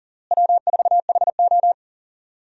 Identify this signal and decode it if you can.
Morse code, heard as a keyed tone, about 40 wpm, 700 Hz. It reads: W4HO
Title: USA Call Signs - 4 Characters - Repeated 2x - 40wpm